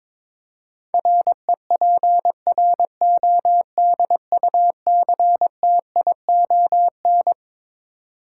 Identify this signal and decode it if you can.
Morse code, heard as a keyed tone, about 22 wpm, 700 Hz. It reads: REPRODUCTION
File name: Twelve Letter Words - 22wpm